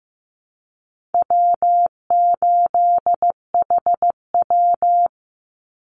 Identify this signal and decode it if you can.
Morse code, heard as a keyed tone, about 15 wpm, 700 Hz. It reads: W8HW
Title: Easy Call Signs Repeated 3x - 15wpm